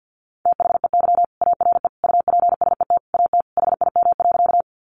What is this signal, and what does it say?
Morse code, heard as a keyed tone, about 50 wpm, 700 Hz. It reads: THEY ARE UPSET AT HIM.
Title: Sentences from Top 800 Words 50wpm